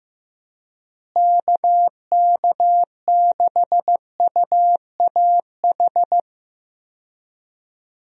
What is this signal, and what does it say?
Morse code, heard as a keyed tone, about 15 wpm, 700 Hz. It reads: KK6UAH
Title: USA Call Signs - 6 Characters - Repeated 3x - 15wpm